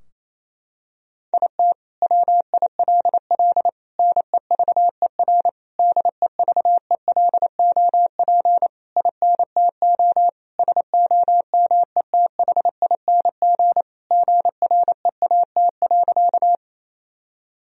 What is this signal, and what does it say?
Morse code, heard as a keyed tone, about 28 wpm, 700 Hz. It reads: IT WILL NEVER DEVELOP INTO SOMETHING GREAT.
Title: Sentences from Top 1000 Words - Review Part 2 - 28wpm